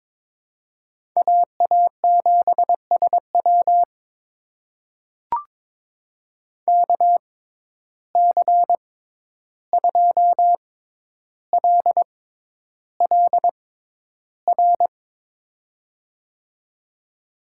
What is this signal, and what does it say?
Morse code, heard as a keyed tone, about 23 wpm, spaced as for 11 wpm, 700 Hz. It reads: AA7SW KC2LLR
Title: USA Call Signs - 6x Character Spacing - 22wpm